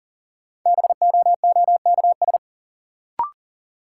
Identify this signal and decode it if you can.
Morse code, heard as a keyed tone, about 39 wpm, 700 Hz. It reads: BOOKS
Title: Sentences from Top 1000 Words - Review Part 4 - New Word Drill - 40wpm